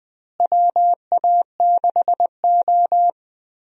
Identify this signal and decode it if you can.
Morse code, heard as a keyed tone, about 20 wpm, 700 Hz. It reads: WA6O
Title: USA Call Signs - 4 Characters - Repeated 2x - 20wpm